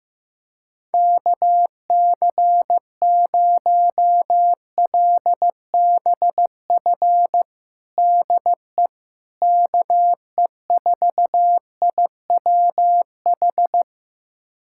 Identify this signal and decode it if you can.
Morse code, heard as a keyed tone, about 15 wpm, 700 Hz. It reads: KC0LBF DE KE4IWH
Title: USA Callsign DE Callsign - 6 Characters - 15wpm